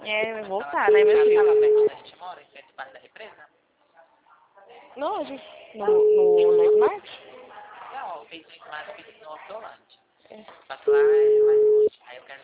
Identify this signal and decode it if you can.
Ringback tone (Europe, 425 Hz), with speech.
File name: out- -2005- - - .0